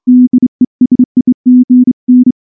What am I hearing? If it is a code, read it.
Morse code, heard as a keyed tone, about 24 wpm, 250 Hz. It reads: DESIGN
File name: morse-training-25WPM